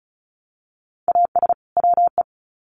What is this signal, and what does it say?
Morse code, heard as a keyed tone, about 34 wpm, 700 Hz. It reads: AS WE
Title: Sets of 2 Words 35wpm